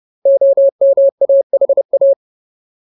Morse code, heard as a keyed tone, about 30 wpm, 550 Hz. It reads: OMAHA
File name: Ultimate Word List / 30WPM Ultimate